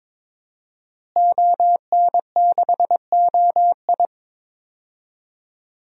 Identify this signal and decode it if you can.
Morse code, heard as a keyed tone, about 22 wpm, 700 Hz. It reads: ON6OI
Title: DXCC-100 Call Signs - 5 Characters - Repeated 2x - 22wpm